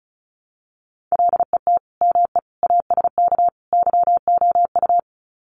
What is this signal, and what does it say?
Morse code, heard as a keyed tone, about 34 wpm, 700 Hz. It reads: LET ME ASK YOU
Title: Sets of 4 Words 35wpm